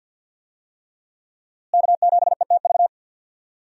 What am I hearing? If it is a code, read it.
Morse code, heard as a keyed tone, about 46 wpm, 700 Hz. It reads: K7ETV
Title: USA Call Signs - 5 Characters - Repeated 3x - 50wpm